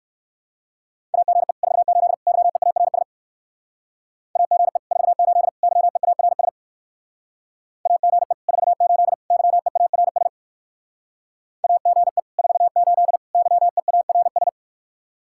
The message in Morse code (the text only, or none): AGE 48 YEARS AGE 48 YEARS AGE 48 YEARS AGE 48 YEARS